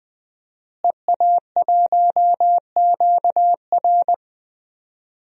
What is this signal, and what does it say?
Morse code, heard as a keyed tone, about 20 wpm, 700 Hz. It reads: EA1QR
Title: DXCC-100 Call Signs - 5 Characters - Repeated 2x - 20wpm